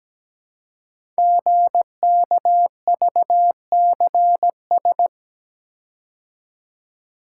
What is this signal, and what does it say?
Morse code, heard as a keyed tone, about 17 wpm, 700 Hz. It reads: GKVCS
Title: Code Groups - 5 Letters - 17wpm